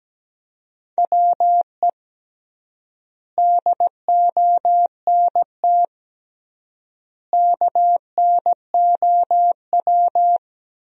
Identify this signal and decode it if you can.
Morse code, heard as a keyed tone, about 17 wpm, 700 Hz. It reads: WE DONT KNOW
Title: 3x Word-Spacing - Sentences from Top 400 Words - 17wpm